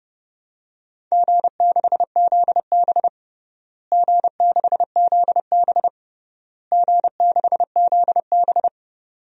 Morse code, heard as a keyed tone, about 30 wpm, 700 Hz. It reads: G6ZB G6ZB G6ZB
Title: DXCC-100 Call Signs - 4 Characters - Repeated 3x - 30wpm